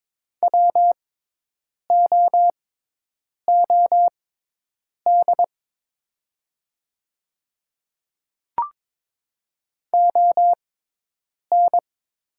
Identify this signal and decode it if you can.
Morse code, heard as a keyed tone, about 23 wpm, spaced as for 8 wpm, 690 Hz. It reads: WOOD ON
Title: 6x Character Spacing - Top 500 Words - 22wpm